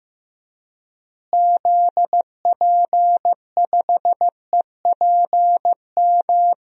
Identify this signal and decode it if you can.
Morse code, heard as a keyed tone, about 15 wpm, 700 Hz. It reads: ZP5EPM
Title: DXCC-100 Call Signs - 6 Characters - Repeated 2x - 15wpm